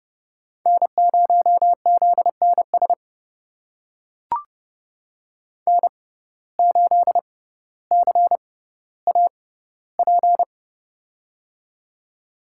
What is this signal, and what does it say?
Morse code, heard as a keyed tone, about 31 wpm, spaced as for 15 wpm, 700 Hz. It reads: N0ZNS N8CAP
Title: USA Call Signs - 5 Characters - 6x Character Spacing - 30wpm